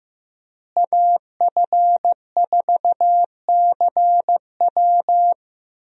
Morse code, heard as a keyed tone, about 15 wpm, 700 Hz. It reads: AF4CW
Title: USA Call Signs - 5 Characters - Repeated 2x - 15wpm